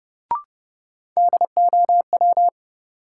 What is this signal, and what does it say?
Morse code, heard as a keyed tone, about 30 wpm, 700 Hz. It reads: DOW
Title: Sets of 5 Words - New Word Drill - 30wpm